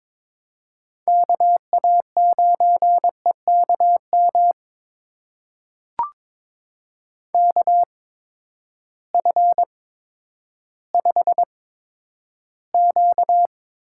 Morse code, heard as a keyed tone, about 22 wpm, spaced as for 11 wpm, 700 Hz. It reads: KA9EKM KF5Q
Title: USA Call Signs - 6 Characters - 8x Character Spacing - 22wpm